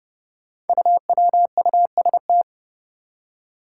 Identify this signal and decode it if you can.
Morse code, heard as a keyed tone, about 30 wpm, 700 Hz. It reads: UWUST
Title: Five Challenging Letter-Number Combinations 30wpm